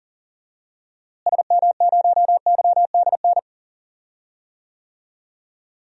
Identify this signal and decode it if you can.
Morse code, heard as a keyed tone, about 37 wpm, 700 Hz. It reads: SM0YDN
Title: DXCC-100 Call Signs - 6 Characters - Repeated 2x - 40wpm